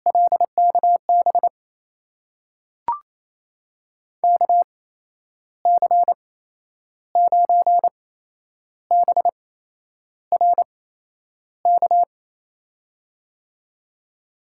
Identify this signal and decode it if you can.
Morse code, heard as a keyed tone, about 28 wpm, spaced as for 10 wpm, 700 Hz. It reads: LKB KC9BRK
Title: USA Call Signs - 6 Characters - 8x Character Spacing - 28wpm